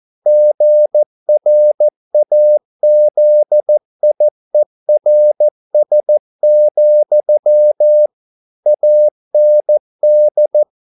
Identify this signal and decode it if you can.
Morse code, heard as a keyed tone, about 14 wpm, 600 Hz. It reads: GRAZIERS, AND